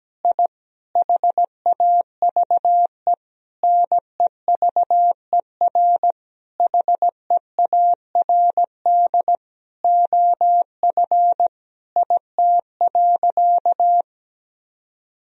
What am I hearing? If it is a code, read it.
Morse code, heard as a keyed tone, about 17 wpm, 700 Hz. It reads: I HAVE NEVER HEARD OF IT.
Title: Sentences from Top 1000 Words - Review Part 3 - 17wpm